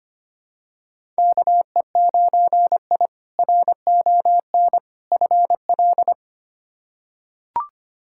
Morse code, heard as a keyed tone, about 25 wpm, 700 Hz. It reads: KE9I RON FL
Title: K1USN SST QSO Practice - 25wpm